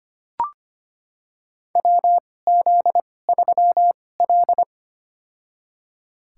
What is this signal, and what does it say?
Morse code, heard as a keyed tone, about 25 wpm, spaced as for 18 wpm, 700 Hz. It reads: WZ3L
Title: USA Call Signs - 4 Characters - 2x Character Spacing - 25wpm